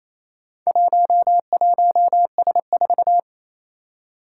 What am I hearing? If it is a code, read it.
Morse code, heard as a keyed tone, about 27 wpm, 700 Hz. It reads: 11S4
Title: Four Challenging Letter-Number Combinations 28wpm